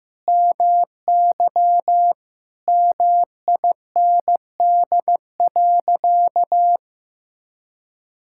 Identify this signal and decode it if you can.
Morse code, heard as a keyed tone, about 15 wpm, 700 Hz. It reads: MY MIND.
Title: Sentences from Top 1000 Words - Review Part 6 - 15wpm